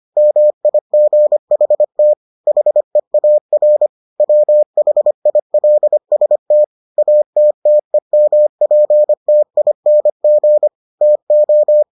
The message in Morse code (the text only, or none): MIGHT HEAR WHILST ATTEMPTING TO